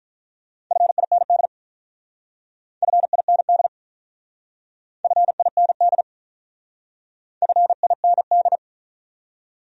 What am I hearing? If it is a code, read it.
Morse code, heard as a keyed tone, about 44 wpm, 700 Hz. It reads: FIND FIND FIND FIND